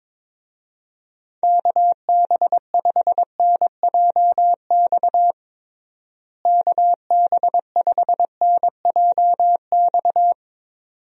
Morse code, heard as a keyed tone, about 22 wpm, 700 Hz. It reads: KB5NJX KB5NJX
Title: USA Call Signs - 6 Characters - Repeated 2x - 22wpm